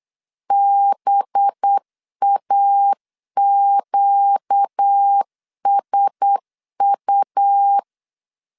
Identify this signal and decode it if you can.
Morse code, heard as a keyed tone, about 8 wpm, 800 Hz. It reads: BAQSU
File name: 8wpm